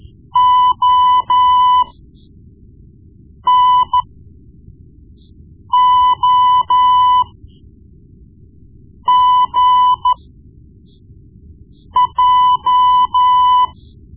Morse code, heard as a keyed tone, about 13 wpm, spaced as for 4 wpm, 1000 Hz. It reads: ONOGJ
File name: Morse Lessons / Morse Lessons / Tape-1B